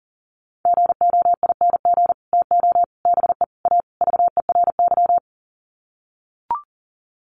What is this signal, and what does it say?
Morse code, heard as a keyed tone, about 40 wpm, 700 Hz. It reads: GOING TO BE A VERY
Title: Sets of 5 Words Encore 40wpm